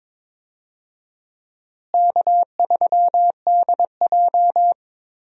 Morse code, heard as a keyed tone, about 21 wpm, 700 Hz. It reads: K3DJ
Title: USA Call Signs - 4 Characters - Repeated 2x - 22wpm